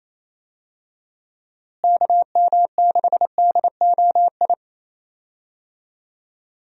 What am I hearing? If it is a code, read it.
Morse code, heard as a keyed tone, about 28 wpm, 700 Hz. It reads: KM6DOI
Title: USA Call Signs - 6 Characters - Repeated 3x - 28wpm